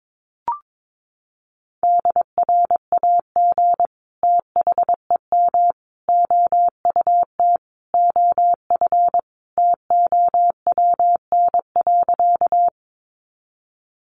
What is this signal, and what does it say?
Morse code, heard as a keyed tone, about 22 wpm, 700 Hz. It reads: DRAG THEM OUT OF TOWN.
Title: Sentences from Top 1500 Words 22wpm